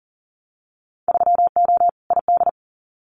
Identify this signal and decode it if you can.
Morse code, heard as a keyed tone, about 40 wpm, 700 Hz. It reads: 3O ID
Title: Winter Field Day Exchanges 40wpm